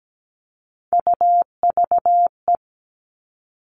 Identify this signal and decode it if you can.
Morse code, heard as a keyed tone, about 17 wpm, 700 Hz. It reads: UVE
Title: All Cut Numbers - 3-Digit 17wpm